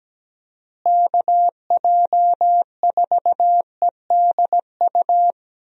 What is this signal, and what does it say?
Morse code, heard as a keyed tone, about 17 wpm, 700 Hz. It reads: KJ4EDU
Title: USA Call Signs - 6 Characters - Repeated 3x - 17wpm